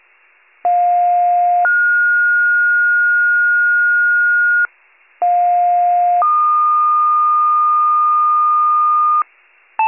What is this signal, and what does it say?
Two-tone page, heard as a one-second tone, then a three-second tone, about 710 then 1430 Hz, about 710 then 1220 Hz, with noise. It is heard from 0.6 to 9.2 s.